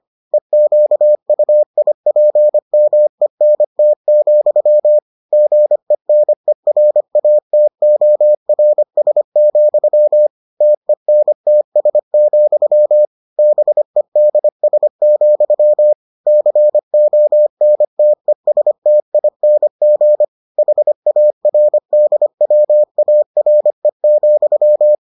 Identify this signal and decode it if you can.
Morse code, heard as a keyed tone, about 25 wpm, 600 Hz. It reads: EQUIPMENT, GENERATORS, TENTS, BEDS, CONTESTING HARDWARE,